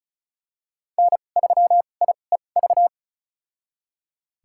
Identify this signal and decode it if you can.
Morse code, heard as a keyed tone, about 34 wpm, spaced as for 25 wpm, 700 Hz. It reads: N3IEV
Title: USA Call Signs - 2x Character Spacing - 35wpm